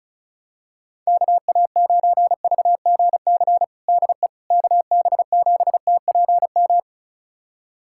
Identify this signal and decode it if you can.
Morse code, heard as a keyed tone, about 36 wpm, 700 Hz. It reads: KA9VGC DE KB7TPM
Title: USA Callsign DE Callsign - 6 Characters - 35wpm